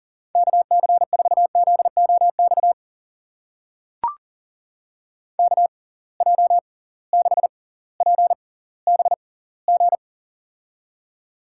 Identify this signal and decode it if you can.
Morse code, heard as a keyed tone, about 40 wpm, spaced as for 20 wpm, 700 Hz. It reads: KC4ZOX KJ6PBG
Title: USA Call Signs - 6 Characters - 6x Character Spacing - 40wpm